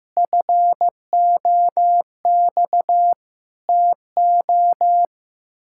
Morse code, heard as a keyed tone, about 15 wpm, 700 Hz. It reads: FOX TO
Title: Pangram Challenge v2 - 15wpm